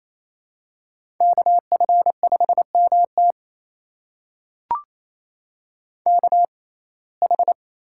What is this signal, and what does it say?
Morse code, heard as a keyed tone, about 28 wpm, spaced as for 20 wpm, 700 Hz. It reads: KF5MT KH
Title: USA Call Signs - 5 Characters - 6x Character Spacing - 28wpm